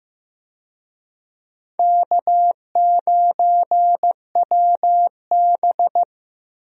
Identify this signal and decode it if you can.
Morse code, heard as a keyed tone, about 15 wpm, 700 Hz. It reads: K9WB
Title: USA Call Signs - 4 Characters - Repeated 2x - 15wpm